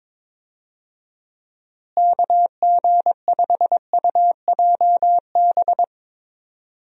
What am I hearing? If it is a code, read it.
Morse code, heard as a keyed tone, about 22 wpm, 700 Hz. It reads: KG5UJB
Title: USA Call Signs - 6 Characters - Repeated 3x - 22wpm